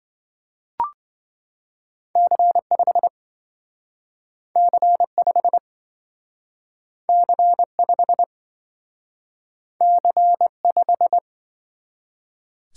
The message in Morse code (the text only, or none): C5 C5 C5 C5